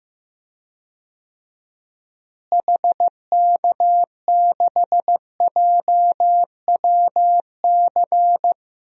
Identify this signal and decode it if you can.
Morse code, heard as a keyed tone, about 15 wpm, 700 Hz. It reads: HK6JWC
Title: DXCC-100 Call Signs - 6 Characters - Repeated 3x - 15wpm